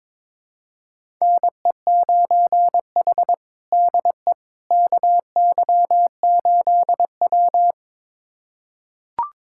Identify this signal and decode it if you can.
Morse code, heard as a keyed tone, about 22 wpm, 700 Hz. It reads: NE9H DE KY8W
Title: USA Callsign DE Callsign - 4 Characters - 22wpm